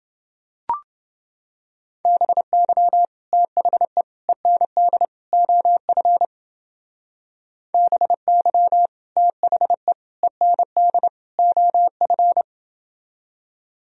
Morse code, heard as a keyed tone, about 31 wpm, 700 Hz. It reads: BY THE END OF BY THE END OF